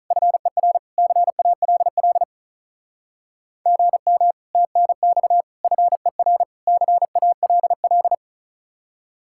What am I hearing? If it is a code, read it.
Morse code, heard as a keyed tone, about 43 wpm, 700 Hz. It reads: FER CALL GM TNX FER CALL
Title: Speed-Racing Common QSO Phrases - 35wpm